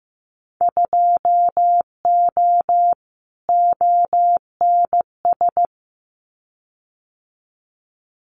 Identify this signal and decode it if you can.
Morse code, heard as a keyed tone, about 15 wpm, 700 Hz. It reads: 2O ONS
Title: Winter Field Day Exchanges Encore 15wpm